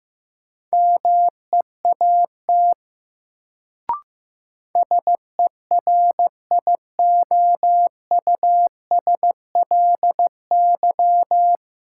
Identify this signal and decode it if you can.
Morse code, heard as a keyed tone, about 15 wpm, 700 Hz. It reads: MEAT SERIOUSLY
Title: Sentences from Top 1200 Words - New Word Drill Alt - 15wpm